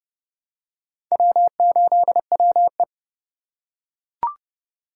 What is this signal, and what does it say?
Morse code, heard as a keyed tone, about 29 wpm, 700 Hz. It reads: W8WE